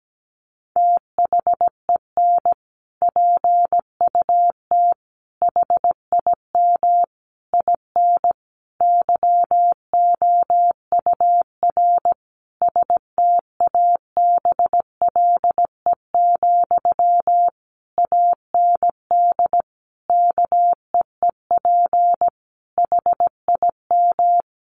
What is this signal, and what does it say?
Morse code, heard as a keyed tone, about 17 wpm, 700 Hz. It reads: THEN PUT HIM IN YOUR STABLE, AND KEEP HIM
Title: Sentences from Top 1500 Words 17wpm